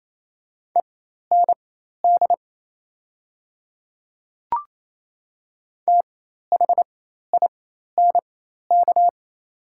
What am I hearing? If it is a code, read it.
Morse code, heard as a keyed tone, about 28 wpm, spaced as for 13 wpm, 690 Hz. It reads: END THINK
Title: 4x Character Spacing - Top 200 Words - 28wpm